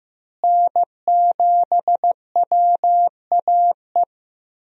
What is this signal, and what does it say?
Morse code, heard as a keyed tone, about 15 wpm, 700 Hz. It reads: N7WAE
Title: USA Call Signs - 5 Characters - Repeated 2x - 15wpm